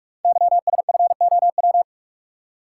Morse code, heard as a keyed tone, about 44 wpm, 690 Hz. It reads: YSFOW
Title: Code Groups - 5 Letter:Numbers - 45wpm